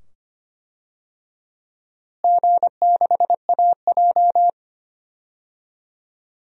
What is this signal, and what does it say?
Morse code, heard as a keyed tone, about 25 wpm, 700 Hz. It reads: G6AJ
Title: DXCC-100 Call Signs - 4 Characters - Repeated 2x - 25wpm